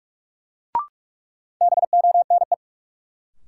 Morse code, heard as a keyed tone, about 46 wpm, 700 Hz. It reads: BONE
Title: Sentences from Top 1600 Words - New Word Drill Alt - 45wpm